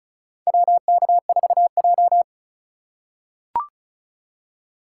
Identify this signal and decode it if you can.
Morse code, heard as a keyed tone, about 36 wpm, 700 Hz. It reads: WK4J
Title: USA Call Signs - 4 Characters - 4x Character Spacing - 35wpm